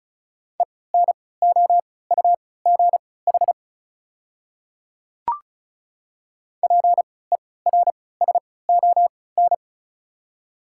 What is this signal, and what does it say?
Morse code, heard as a keyed tone, about 34 wpm, spaced as for 20 wpm, 700 Hz. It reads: ENOUGH PERSON
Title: 3x Character Spacing - Top 500 Words - 35wpm